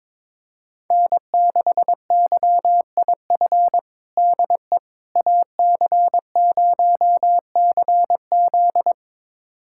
Morse code, heard as a keyed tone, about 21 wpm, 700 Hz. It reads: N6YIF DE AC0CZ
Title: USA Callsign DE Callsign - 5 Characters - 22wpm